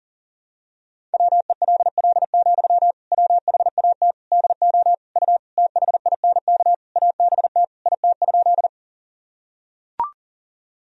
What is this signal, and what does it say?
Morse code, heard as a keyed tone, about 40 wpm, 700 Hz. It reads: WELL, WHAT DO U THINK ABT IT?
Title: Sentences from Top 200 Words and Abbreviations 40wpm